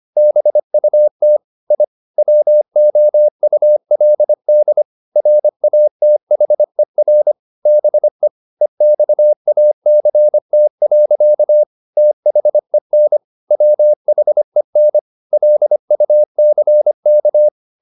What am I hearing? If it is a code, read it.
Morse code, heard as a keyed tone, about 25 wpm, 600 Hz. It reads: BUT I WOULD RATHER BE EXACT. THEN WHEN LUCK
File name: the old man and the sea part 1 25wpm